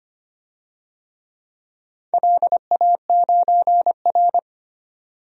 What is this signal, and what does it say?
Morse code, heard as a keyed tone, about 25 wpm, 700 Hz. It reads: LA9R
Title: DXCC-100 Call Signs - 4 Characters - 25wpm